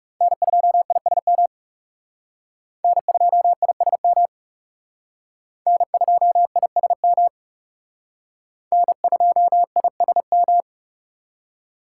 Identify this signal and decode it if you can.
Morse code, heard as a keyed tone, about 39 wpm, 700 Hz. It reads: N2ISM N2ISM N2ISM N2ISM